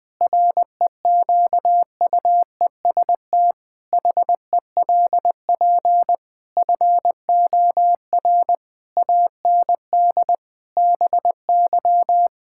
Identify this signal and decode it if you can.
Morse code, heard as a keyed tone, about 20 wpm, 700 Hz. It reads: REQUEST HELP FOR AND BY
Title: Pangram Challenge v2 - 20wpm